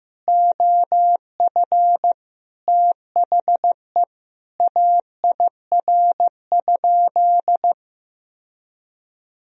Morse code, heard as a keyed tone, about 15 wpm, 700 Hz. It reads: OF THE AIR?
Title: Sentences from Top 1000 Words - Review Part 2 - 15wpm